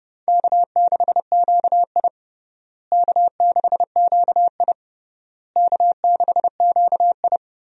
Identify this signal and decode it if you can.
Morse code, heard as a keyed tone, about 30 wpm, 700 Hz. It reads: K6QI K6QI K6QI
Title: USA Call Signs - 4 Characters - Repeated 3x - 30wpm